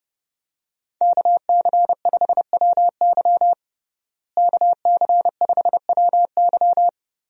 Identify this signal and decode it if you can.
Morse code, heard as a keyed tone, about 30 wpm, 700 Hz. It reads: KC5WY KC5WY
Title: USA Call Signs - 5 Characters - Repeated 3x - 30wpm